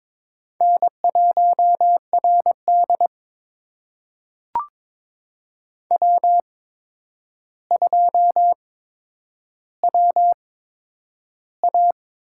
Morse code, heard as a keyed tone, about 21 wpm, spaced as for 9 wpm, 700 Hz. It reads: N1RD W2WA
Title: USA Call Signs - 4 Characters - 8x Character Spacing - 22wpm